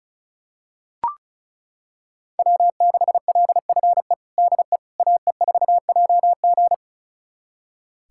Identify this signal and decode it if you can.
Morse code, heard as a keyed tone, about 36 wpm, 700 Hz. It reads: W6LFE DE AE4JG
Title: USA Callsign DE Callsign - 5 Characters - 35wpm